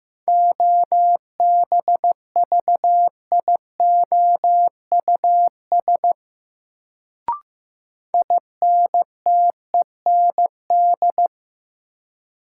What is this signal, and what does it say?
Morse code, heard as a keyed tone, about 15 wpm, 700 Hz. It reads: OBVIOUS INTEND
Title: Sentences from Top 1600 Words - New Word Drill Alt - 15wpm